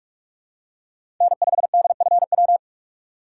Morse code, heard as a keyed tone, about 46 wpm, 700 Hz. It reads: N5DFW
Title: USA Call Signs - 5 Characters - 2x Character Spacing - 45wpm